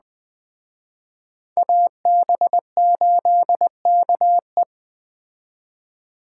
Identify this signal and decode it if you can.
Morse code, heard as a keyed tone, about 19 wpm, 700 Hz. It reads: AB8KE